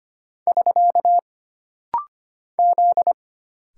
Morse code, heard as a keyed tone, about 25 wpm, 700 Hz. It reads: <SK> Z
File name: Single Character (Letter:Number:Punctuation:Prosign) - Rapid-Fire - 25wpm